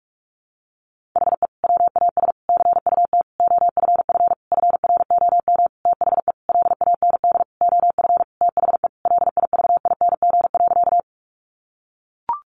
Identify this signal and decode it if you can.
Morse code, heard as a keyed tone, about 44 wpm, 700 Hz. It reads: HE WAS CUT OFF FROM THE LAND OF THE LIVING.
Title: Sentences from Top 1100 Words 45wpm